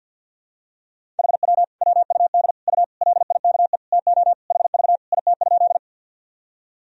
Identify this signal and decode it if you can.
Morse code, heard as a keyed tone, about 47 wpm, 700 Hz. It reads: HW WUD U LIKE TO HV IT?
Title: Sentences from Top 100 Words and Abbreviations 50wpm